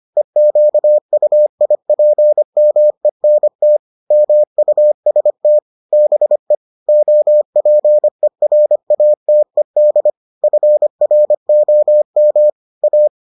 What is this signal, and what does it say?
Morse code, heard as a keyed tone, about 25 wpm, 600 Hz. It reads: EQUIPMENT MUST BE OPERATED FROM A